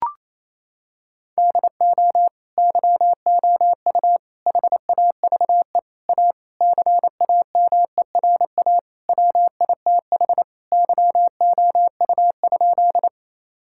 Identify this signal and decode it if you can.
Morse code, heard as a keyed tone, about 28 wpm, 700 Hz. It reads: DO YOU HAVE A CAMERA WITH YOU?
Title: Sentences from Top 1000 Words - Review Part 5 - 28wpm